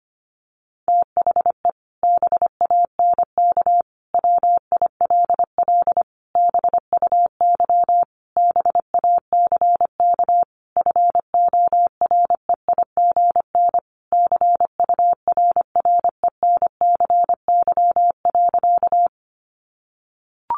Morse code, heard as a keyed tone, about 25 wpm, 700 Hz. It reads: THE BANK WILL BUY BACK FOREIGN CURRENCY.
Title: Sentences from Top 1500 Words 25wpm